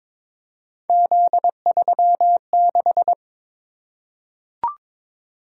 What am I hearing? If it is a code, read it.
Morse code, heard as a keyed tone, about 22 wpm, 700 Hz. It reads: Z36
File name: practice / 22wpm / Three Letter-Number Combinations - 22wpm